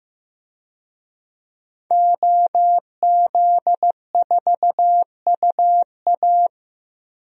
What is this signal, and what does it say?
Morse code, heard as a keyed tone, about 15 wpm, 700 Hz. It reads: OZ4UA
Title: DXCC-100 Call Signs - 5 Characters - Repeated 2x - 15wpm